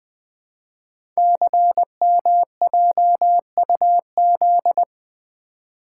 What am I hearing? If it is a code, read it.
Morse code, heard as a keyed tone, about 20 wpm, 700 Hz. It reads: CMJUZ
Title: Code Groups - 5 Letters - 20wpm